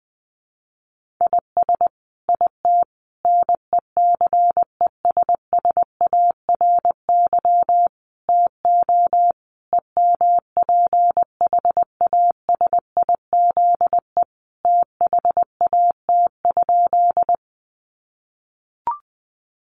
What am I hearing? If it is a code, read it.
Morse code, heard as a keyed tone, about 20 wpm, 700 Hz. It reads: IS IT NECESSARY TO EMPHASIZE THAT?
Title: Sentences from Top 1500 Words 20wpm